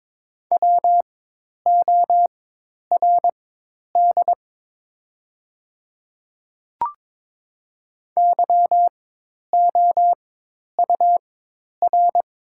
Morse code, heard as a keyed tone, about 22 wpm, spaced as for 10 wpm, 700 Hz. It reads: WORD YOUR
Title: 4x Character Spacing - Top 100 Words - 22wpm